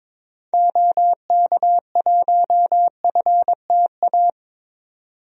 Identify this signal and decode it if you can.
Morse code, heard as a keyed tone, about 22 wpm, 700 Hz. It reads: OK1FTA
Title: DXCC-100 Call Signs - 6 Characters - 22wpm